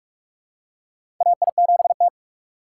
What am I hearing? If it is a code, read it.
Morse code, heard as a keyed tone, about 42 wpm, 700 Hz. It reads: AI7T
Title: USA Call Signs - 3x Character Spacing - 45wpm